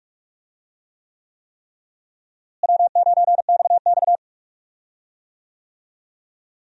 Morse code, heard as a keyed tone, about 46 wpm, 700 Hz. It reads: W9XX
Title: USA Call Signs - 4 Characters - Repeated 3x - 45wpm